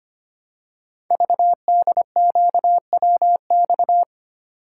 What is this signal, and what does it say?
Morse code, heard as a keyed tone, about 25 wpm, 700 Hz. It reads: VDQWX
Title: Code Groups - 5 Letter:Numbers - 25wpm